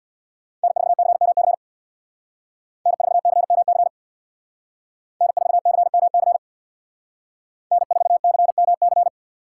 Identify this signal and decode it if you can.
Morse code, heard as a keyed tone, about 63 wpm, 700 Hz. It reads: N4CMC N4CMC N4CMC N4CMC